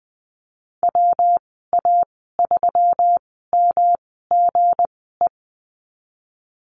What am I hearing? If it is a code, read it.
Morse code, heard as a keyed tone, about 20 wpm, spaced as for 14 wpm, 700 Hz. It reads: WA3MGE
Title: 193 Lesson 31g - Letter v - Call Signs 20wpm